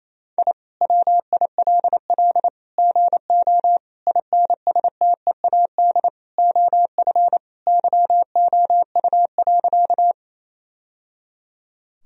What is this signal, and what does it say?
Morse code, heard as a keyed tone, about 28 wpm, 700 Hz. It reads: I WILL GO INSTEAD OF YOU.
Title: Sentences from Top 1000 Words - Review Part 4 - 28wpm